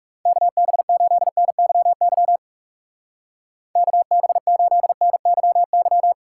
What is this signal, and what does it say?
Morse code, heard as a keyed tone, about 37 wpm, 700 Hz. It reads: KB8NYY KB8NYY
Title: Speed Racing - USA 2x3 Call Signs 30wpm